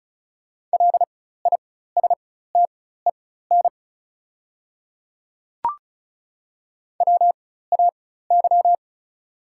Morse code, heard as a keyed tone, about 36 wpm, spaced as for 16 wpm, 690 Hz. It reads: LISTEN WAY
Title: 4x Character Spacing - Top 500 Words - 35wpm